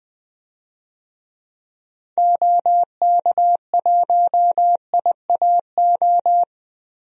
Morse code, heard as a keyed tone, about 20 wpm, 700 Hz. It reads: OK1IAO
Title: DXCC-100 Call Signs - 6 Characters - Repeated 3x - 20wpm